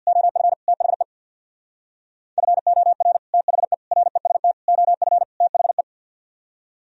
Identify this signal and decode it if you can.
Morse code, heard as a keyed tone, about 64 wpm, 700 Hz. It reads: OF THE FOR THE REST OF THE